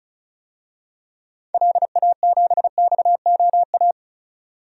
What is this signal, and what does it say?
Morse code, heard as a keyed tone, about 36 wpm, 700 Hz. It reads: LA7XOA
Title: DXCC-100 Call Signs - 6 Characters - 35wpm